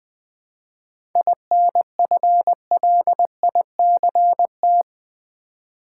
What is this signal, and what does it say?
Morse code, heard as a keyed tone, about 20 wpm, 700 Hz. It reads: INFLICT
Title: 7 Letter Words - First 3 Letters Given - 20wpm